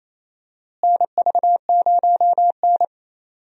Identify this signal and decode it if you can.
Morse code, heard as a keyed tone, about 28 wpm, 700 Hz. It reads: NV0N
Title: USA Call Signs - 4 Characters - Repeated 3x - 28wpm